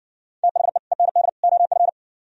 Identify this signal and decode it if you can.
Morse code, heard as a keyed tone, about 58 wpm, 700 Hz. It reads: THE END OF